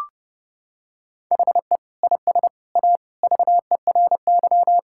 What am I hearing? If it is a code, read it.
Morse code, heard as a keyed tone, about 32 wpm, 700 Hz. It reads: HE IS A VERY
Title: Sentences from Top 1000 Words - Review Part 5 - 30wpm